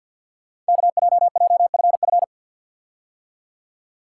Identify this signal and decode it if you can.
Morse code, heard as a keyed tone, about 48 wpm, 690 Hz. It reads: KJJFF
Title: Code Groups - 5 Letters - 50wpm